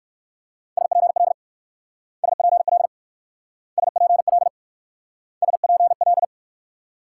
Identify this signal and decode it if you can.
Morse code, heard as a keyed tone, about 57 wpm, 690 Hz. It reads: SPL SPL SPL SPL